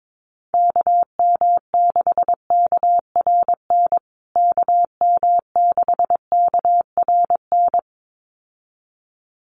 Morse code, heard as a keyed tone, about 22 wpm, 700 Hz. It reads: KM6KRN KM6KRN
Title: USA Call Signs Repeated 2x 22wpm